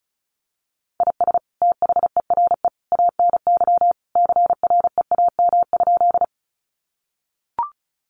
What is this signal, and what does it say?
Morse code, heard as a keyed tone, about 34 wpm, 700 Hz. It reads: IS THERE ANY CREAM?
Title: Sentences from Top 1500 Words 35wpm